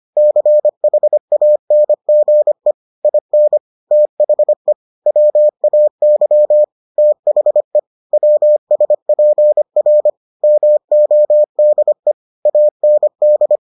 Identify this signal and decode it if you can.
Morse code, heard as a keyed tone, about 25 wpm, 600 Hz. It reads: CHANGE IN THE WAY THE WSPR MODE AND